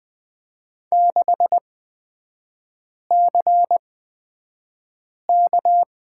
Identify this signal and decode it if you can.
Morse code, heard as a keyed tone, about 20 wpm, spaced as for 5 wpm, 700 Hz. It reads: 6CK
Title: 250 Lesson 40e - BK - Three Characters 20wpm